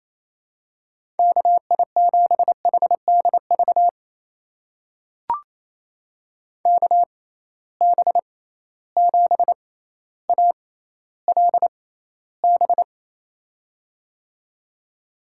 Morse code, heard as a keyed tone, about 28 wpm, spaced as for 14 wpm, 700 Hz. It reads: KI7HDV KB7ALB
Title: USA Call Signs - 6x Character Spacing - 28wpm